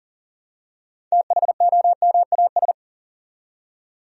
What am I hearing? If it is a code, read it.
Morse code, heard as a keyed tone, about 40 wpm, 700 Hz. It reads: THOMAS
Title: USA Common Names - 40wpm